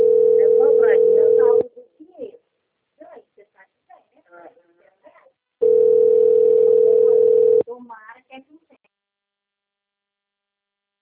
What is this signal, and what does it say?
Ringback tone (North America), 440 + 480 Hz, with speech.
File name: rg-1600-unknown- - - .97041